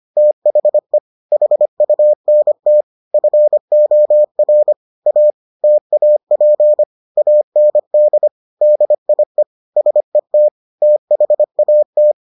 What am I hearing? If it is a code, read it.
Morse code, heard as a keyed tone, about 25 wpm, 600 Hz. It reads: THE HUNT FOR A TAP AND DIE SET THAT